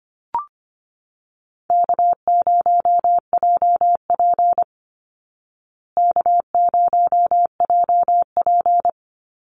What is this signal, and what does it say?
Morse code, heard as a keyed tone, about 25 wpm, 700 Hz. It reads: K0JP K0JP
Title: Call Signs Repeated 3x - 25wpm